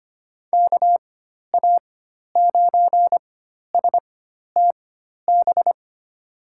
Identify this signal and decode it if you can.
Morse code, heard as a keyed tone, about 25 wpm, spaced as for 12 wpm, 700 Hz. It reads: KA9STB
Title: USA Call Signs - 6 Characters - 4x Character Spacing - 25wpm